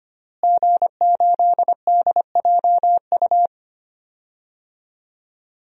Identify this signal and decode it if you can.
Morse code, heard as a keyed tone, about 25 wpm, 700 Hz. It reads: G8DJU
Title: DXCC-100 Call Signs - 5 Characters - Repeated 3x - 25wpm